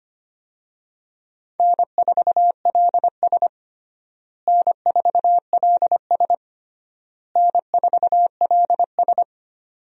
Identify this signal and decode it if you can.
Morse code, heard as a keyed tone, about 25 wpm, 700 Hz. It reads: N4LS N4LS N4LS
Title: USA Call Signs - 4 Characters - Repeated 3x - 25wpm